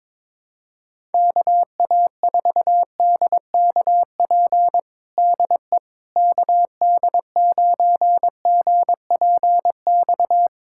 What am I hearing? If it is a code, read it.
Morse code, heard as a keyed tone, about 22 wpm, 700 Hz. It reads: KA4DKP DE KD9GPX
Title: USA Callsign DE Callsign 22wpm